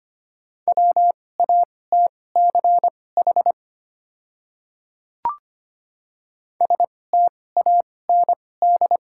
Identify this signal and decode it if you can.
Morse code, heard as a keyed tone, about 25 wpm, spaced as for 18 wpm, 700 Hz. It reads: WATCH STAND
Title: 2x Character Spacing - Top 300 Words - 25wpm